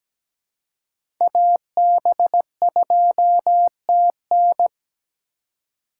Morse code, heard as a keyed tone, about 17 wpm, 700 Hz. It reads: AB2TN
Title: USA Call Signs - 6x Character Spacing - 17wpm